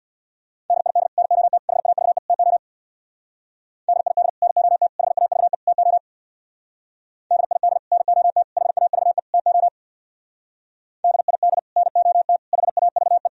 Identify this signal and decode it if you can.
Morse code, heard as a keyed tone, about 61 wpm, 700 Hz. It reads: DID NOT HAVE TO DID NOT HAVE TO DID NOT HAVE TO DID NOT HAVE
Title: Speed-Racing - Sets of 4 Words 50wpm